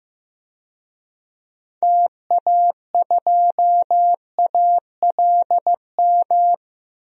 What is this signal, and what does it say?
Morse code, heard as a keyed tone, about 15 wpm, 700 Hz. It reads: TA2ALM
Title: DXCC-100 Call Signs - 6 Characters - 15wpm